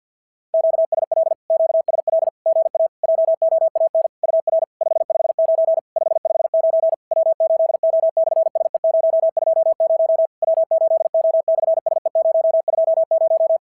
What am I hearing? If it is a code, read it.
Morse code, heard as a keyed tone, about 50 wpm, 690 Hz. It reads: QSL QSL GA JOAN UR 559 559 W8O/SE020 W8O/SE020
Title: SOTA QSO Practice 50wpm